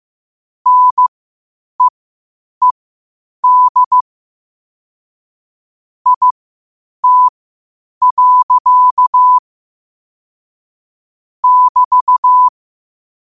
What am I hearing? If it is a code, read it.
Morse code, heard as a keyed tone, about 15 wpm, spaced as for 8 wpm, 1000 Hz. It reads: NEED IT. <BT>
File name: qotdfiles / 8wpm